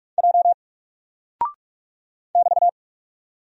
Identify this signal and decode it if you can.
Morse code, heard as a keyed tone, about 45 wpm, 700 Hz. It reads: J <BT>
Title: Single Character (Letter:Number:Punctuation:Prosign) Alt 45wpm